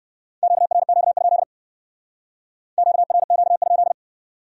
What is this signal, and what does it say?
Morse code, heard as a keyed tone, about 62 wpm, 700 Hz. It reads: QRQ? QRQ?